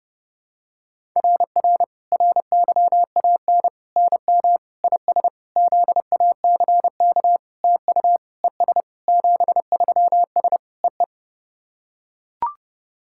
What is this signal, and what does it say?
Morse code, heard as a keyed tone, about 30 wpm, 700 Hz. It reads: RR RYAN NM IS ZACK TU ES 73S EE
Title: SOTA QSO Practice 30wpm